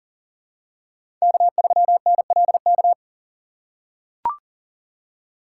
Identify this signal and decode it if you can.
Morse code, heard as a keyed tone, about 37 wpm, 700 Hz. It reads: K3NLK
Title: USA Call Signs - 5 Characters - 4x Character Spacing - 40wpm